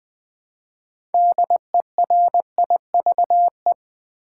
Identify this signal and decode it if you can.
Morse code, heard as a keyed tone, about 20 wpm, 700 Hz. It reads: DERIVE
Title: Build A Word - 2 Letters At A Time - 6-Letter Words - 20wpm